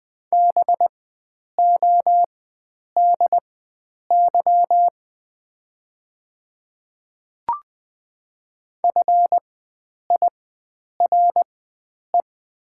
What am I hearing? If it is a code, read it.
Morse code, heard as a keyed tone, about 20 wpm, spaced as for 9 wpm, 700 Hz. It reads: BODY FIRE
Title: 4x Character Spacing - Top 500 Words - 20wpm